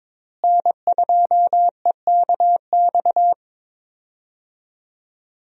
Morse code, heard as a keyed tone, about 21 wpm, 700 Hz. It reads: N2EKX
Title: USA Call Signs - 5 Characters - Repeated 2x - 22wpm